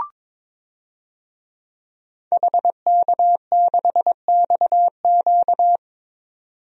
Morse code, heard as a keyed tone, about 22 wpm, 700 Hz. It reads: HK6XQ
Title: DXCC-100 Call Signs - 5 Characters - 22wpm